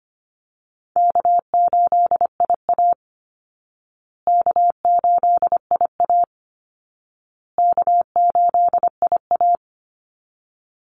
Morse code, heard as a keyed tone, about 25 wpm, 700 Hz. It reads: K8IA K8IA K8IA
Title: Easy Call Signs Repeated 3x - 25wpm